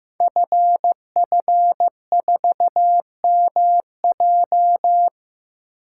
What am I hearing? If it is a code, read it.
Morse code, heard as a keyed tone, about 15 wpm, 700 Hz. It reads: FF4MJ
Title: Code Groups - 5 Letter:Numbers - 15wpm